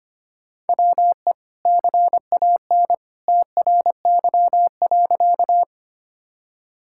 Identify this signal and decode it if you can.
Morse code, heard as a keyed tone, about 25 wpm, 700 Hz. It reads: WE CAN TRY.
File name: practice / 25wpm / Sentences with 2-Word Phrases - 25wpm